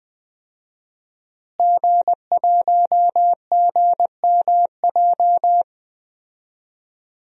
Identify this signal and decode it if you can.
Morse code, heard as a keyed tone, about 20 wpm, 700 Hz. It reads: G1GMJ
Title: DXCC-100 Call Signs - 5 Characters - Repeated 2x - 20wpm